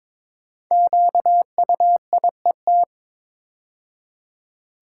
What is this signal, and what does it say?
Morse code, heard as a keyed tone, about 22 wpm, 700 Hz. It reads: QUIET
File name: practice / 22wpm / 5 Letter Words - First 3 Letters Given - 22wpm